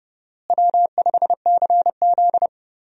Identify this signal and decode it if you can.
Morse code, heard as a keyed tone, about 31 wpm, 700 Hz. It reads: W5CZ
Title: USA Call Signs - 4 Characters - Repeated 2x - 30wpm